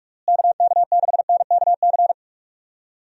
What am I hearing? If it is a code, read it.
Morse code, heard as a keyed tone, about 41 wpm, 700 Hz. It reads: KK6NKC
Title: USA Call Signs - 6 Characters - Repeated 3x - 45wpm